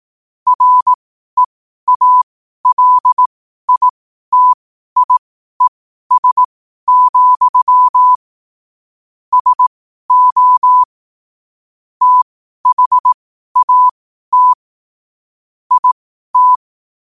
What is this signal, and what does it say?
Morse code, heard as a keyed tone, about 18 wpm, spaced as for 12 wpm, 1000 Hz. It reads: REALITIES, SO THAT IT
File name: qotdfiles / 12wpm